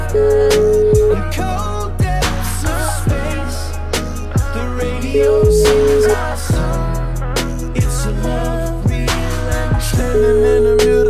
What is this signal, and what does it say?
Ringback tone (Europe, 425 Hz), with noise.